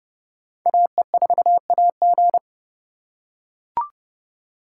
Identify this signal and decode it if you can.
Morse code, heard as a keyed tone, about 29 wpm, 700 Hz. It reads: AE4AG